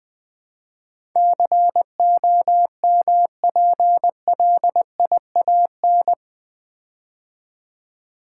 Morse code, heard as a keyed tone, about 21 wpm, 700 Hz. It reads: COMPLIAN
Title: Build A Word - 4 Letters At A Time - 10-Letter Words - 20wpm